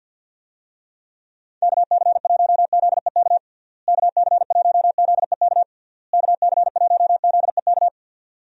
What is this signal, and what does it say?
Morse code, heard as a keyed tone, about 50 wpm, 700 Hz. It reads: KC1ZEK KC1ZEK KC1ZEK
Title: USA Call Signs - 6 Characters - Repeated 3x - 50wpm